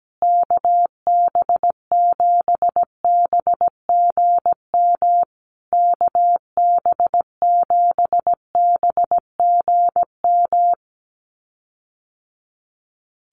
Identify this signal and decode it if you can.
Morse code, heard as a keyed tone, about 17 wpm, 700 Hz. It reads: KB7BGM KB7BGM
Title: USA Call Signs Repeated 3x 17wpm